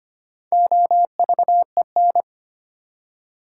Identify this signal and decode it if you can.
Morse code, heard as a keyed tone, about 25 wpm, 700 Hz. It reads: OVEN
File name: practice / 25wpm / Easy Words with Hints - 25wpm